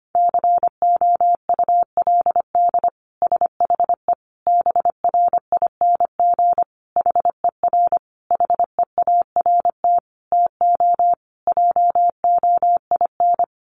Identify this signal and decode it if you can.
Morse code, heard as a keyed tone, about 25 wpm, 700 Hz. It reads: COULD SHE BRING HER HEART TO JOIN
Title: Sentences from Top 1400 Words 25wpm